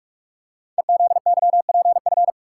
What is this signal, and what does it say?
Morse code, heard as a keyed tone, about 44 wpm, 700 Hz. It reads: EZYPF
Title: Code Groups - 5 Letters - 45wpm